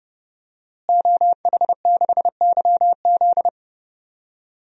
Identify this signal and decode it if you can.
Morse code, heard as a keyed tone, about 30 wpm, 700 Hz. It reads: OH6YZ
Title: DXCC-100 Call Signs - 5 Characters - Repeated 2x - 30wpm